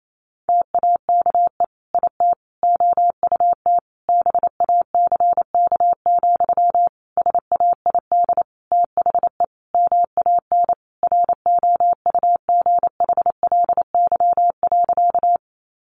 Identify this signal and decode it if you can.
Morse code, heard as a keyed tone, about 28 wpm, 700 Hz. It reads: TAKE IT OUT BACK, SAID THE MAN ROUGHLY.
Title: Sentences from Top 1500 Words 28wpm